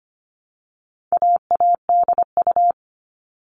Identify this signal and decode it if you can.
Morse code, heard as a keyed tone, about 25 wpm, 700 Hz. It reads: AADU
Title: All Cut Numbers - 4-Digit 25wpm